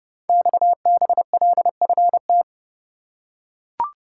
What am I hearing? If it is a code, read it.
Morse code, heard as a keyed tone, about 28 wpm, 700 Hz. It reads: XBLFT
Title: Code Groups - 5 Letters - 30wpm